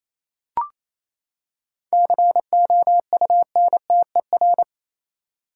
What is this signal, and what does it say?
Morse code, heard as a keyed tone, about 28 wpm, 700 Hz. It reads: COUNTER
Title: Seven Letter Words - 28wpm